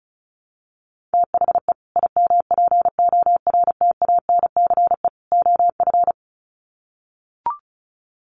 Morse code, heard as a keyed tone, about 34 wpm, 700 Hz. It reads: THE IMPORTANCE OF
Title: Sets of 3 Words 35wpm